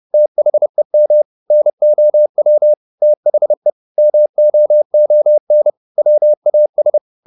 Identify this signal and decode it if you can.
Morse code, heard as a keyed tone, about 30 wpm, 600 Hz. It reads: THEM NOW THE MOON WAS
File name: the old man and the sea / the old man and the sea 1 30wpm